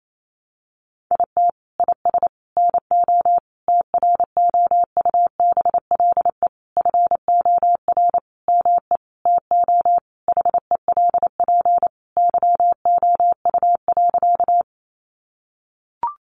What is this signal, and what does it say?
Morse code, heard as a keyed tone, about 28 wpm, 700 Hz. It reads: IT IS NO TROUBLE FOR ME TO HELP YOU.
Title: Sentences from Top 1200 Words 28wpm